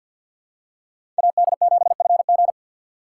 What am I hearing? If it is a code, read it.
Morse code, heard as a keyed tone, about 46 wpm, 690 Hz. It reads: AD7FG
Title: USA Call Signs - 5 Characters - 6x Character Spacing - 50wpm